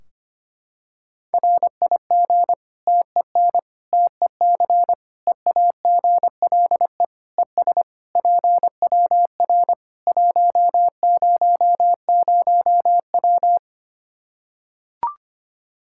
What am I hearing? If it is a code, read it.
Morse code, heard as a keyed tone, about 25 wpm, 700 Hz. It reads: RIG TEN TEC EAGLE ES PWR 100W
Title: Practice QSO - Exchange 2 - Info - 25wpm